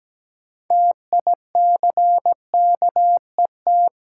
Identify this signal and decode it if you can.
Morse code, heard as a keyed tone, about 17 wpm, 700 Hz. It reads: TICKET
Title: Sentences from Top 1400 Words - New Word Drill - 17wpm